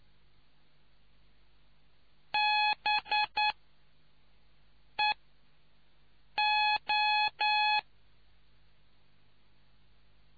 Morse code, heard as a keyed tone, about 9 wpm, 840 Hz. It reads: B E O